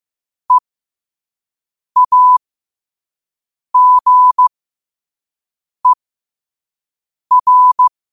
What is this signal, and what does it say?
Morse code, heard as a keyed tone, about 15 wpm, spaced as for 5 wpm, 1000 Hz. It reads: EAGER
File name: qotdfiles / 5wpm